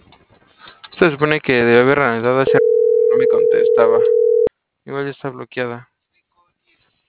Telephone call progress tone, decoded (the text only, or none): ringback tone